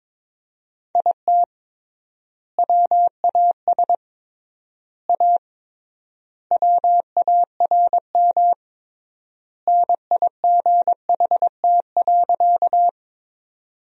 Morse code, heard as a keyed tone, about 22 wpm, 700 Hz. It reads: IT WAS A WARM NIGHT.
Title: 3x Word-Spacing - Sentences from Top 500 Words - 22wpm